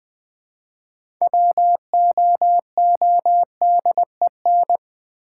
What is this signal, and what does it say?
Morse code, heard as a keyed tone, about 20 wpm, 700 Hz. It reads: WOODEN
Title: Build A Word - 3 Letters At A Time - 6-Letter Words - 20wpm